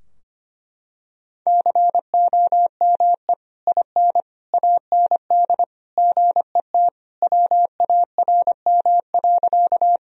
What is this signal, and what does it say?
Morse code, heard as a keyed tone, about 25 wpm, 700 Hz. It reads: COME IN AND GET WARM.
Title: Sentences from Top 1000 Words - Review Part 2 - 25wpm